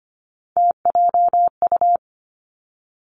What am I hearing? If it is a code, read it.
Morse code, heard as a keyed tone, about 25 wpm, 700 Hz. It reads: TJU
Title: Three Challenging Letter-Number Combinations 25wpm